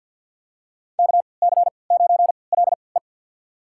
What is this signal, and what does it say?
Morse code, heard as a keyed tone, about 52 wpm, spaced as for 29 wpm, 690 Hz. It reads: KC9LE
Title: USA Call Signs - 3x Character Spacing - 50wpm